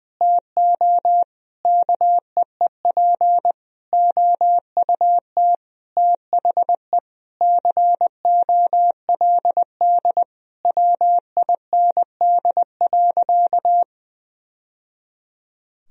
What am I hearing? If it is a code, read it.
Morse code, heard as a keyed tone, about 20 wpm, 700 Hz. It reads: TO KEEP OUT THE COLD WIND.
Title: Sentences from Top 1000 Words - Review Part 2 - 20wpm